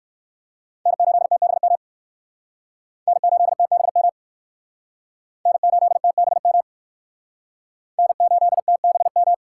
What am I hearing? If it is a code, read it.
Morse code, heard as a keyed tone, about 68 wpm, 700 Hz. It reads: N8TBM N8TBM N8TBM N8TBM